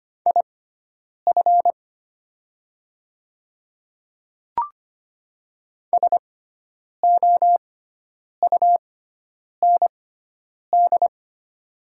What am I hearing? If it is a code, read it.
Morse code, heard as a keyed tone, about 25 wpm, spaced as for 9 wpm, 690 Hz. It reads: IF SOUND
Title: 6x Character Spacing - Top 100 Words - 25wpm